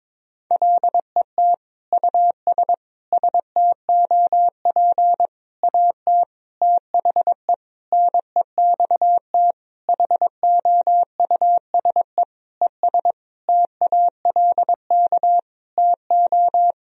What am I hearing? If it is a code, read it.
Morse code, heard as a keyed tone, about 22 wpm, 700 Hz. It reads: LET US STOP AT THE NEXT HOUSE ES TALK TO
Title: Sentences from Top 400 Words and Abbreviations 22wpm